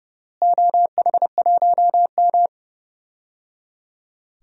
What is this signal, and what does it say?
Morse code, heard as a keyed tone, about 30 wpm, 700 Hz. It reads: OH1M
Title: DXCC-100 Call Signs - 4 Characters - Repeated 3x - 30wpm